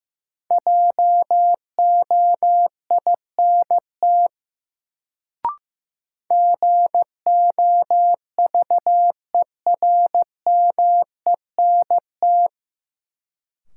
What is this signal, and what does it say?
Morse code, heard as a keyed tone, about 15 wpm, 700 Hz. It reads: JOINT GOVERMENT
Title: Sentences from Top 1300 Words - New Word Drill Alt - 15wpm